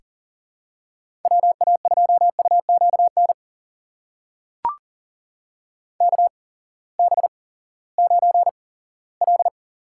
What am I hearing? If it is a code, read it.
Morse code, heard as a keyed tone, about 39 wpm, spaced as for 20 wpm, 700 Hz. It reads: WA2UQN KB9L
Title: USA Call Signs - 6 Characters - 8x Character Spacing - 40wpm